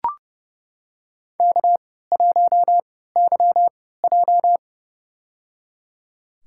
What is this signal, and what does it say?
Morse code, heard as a keyed tone, about 30 wpm, spaced as for 17 wpm, 700 Hz. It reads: K1YJ
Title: USA Call Signs - 4 Characters - 3x Character Spacing - 30wpm